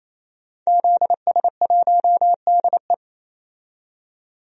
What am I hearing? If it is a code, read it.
Morse code, heard as a keyed tone, about 28 wpm, 700 Hz. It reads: ZS1DE
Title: DXCC-100 Call Signs - 5 Characters - 28wpm